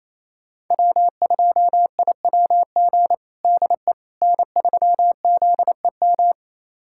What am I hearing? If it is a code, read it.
Morse code, heard as a keyed tone, about 28 wpm, 700 Hz. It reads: W2IWG DE N3ZEM
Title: USA Callsign DE Callsign - 5 Characters - 28wpm